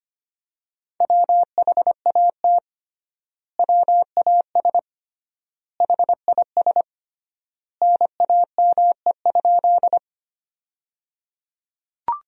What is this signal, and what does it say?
Morse code, heard as a keyed tone, about 25 wpm, 700 Hz. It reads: WHAT WAS HIS NAME?
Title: 3x Word-Spacing - Sentences from Top 200 Words - 25wpm